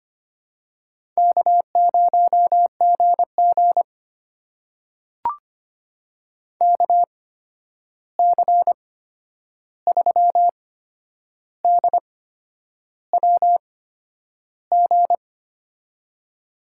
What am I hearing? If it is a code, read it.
Morse code, heard as a keyed tone, about 25 wpm, spaced as for 9 wpm, 700 Hz. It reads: K0GG KC3DWG
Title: USA Call Signs - 8x Character Spacing - 25wpm